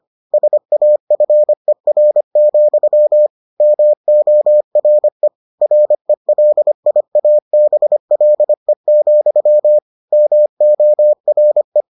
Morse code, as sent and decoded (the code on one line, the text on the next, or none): ... .- ..-. .-. --..-- / -- --- .-. / .-. .-.. .. .- -... .-.. --..-- / -- --- .-. .
SAFER, MORE RELIABLE, MORE